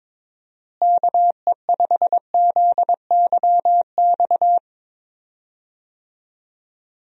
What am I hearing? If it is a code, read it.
Morse code, heard as a keyed tone, about 22 wpm, 700 Hz. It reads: KE5ZYX
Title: USA Call Signs - 6 Characters - Repeated 2x - 22wpm